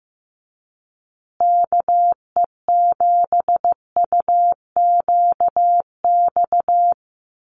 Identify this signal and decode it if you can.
Morse code, heard as a keyed tone, about 15 wpm, 700 Hz. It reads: KE7UQX
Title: USA Call Signs Repeated 2x 15wpm